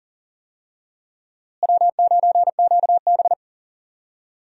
Morse code, heard as a keyed tone, about 40 wpm, 700 Hz. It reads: W9QB
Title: USA Call Signs - 4 Characters - Repeated 2x - 40wpm